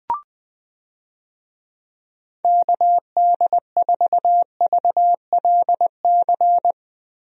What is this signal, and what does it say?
Morse code, heard as a keyed tone, about 20 wpm, 700 Hz. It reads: KD4VLC
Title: USA Call Signs - 6 Characters - Repeated 2x - 20wpm